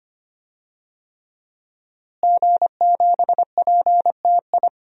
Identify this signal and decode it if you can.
Morse code, heard as a keyed tone, about 25 wpm, 700 Hz. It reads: G7PTI
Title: DXCC-100 Call Signs - 5 Characters - Repeated 3x - 25wpm